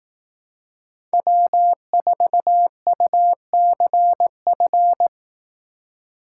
Morse code, heard as a keyed tone, about 18 wpm, 700 Hz. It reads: W4UCF
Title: Speed Racing - USA 1x3 Call Signs 15wpm